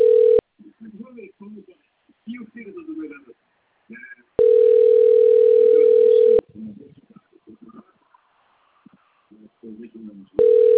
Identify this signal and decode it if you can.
Ringback tone (North America), 440 + 480 Hz, with speech.
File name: out- - - - - .64870